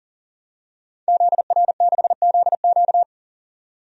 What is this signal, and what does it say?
Morse code, heard as a keyed tone, about 37 wpm, 700 Hz. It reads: ZR6ZQ
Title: DXCC-100 Call Signs - 5 Characters - 40wpm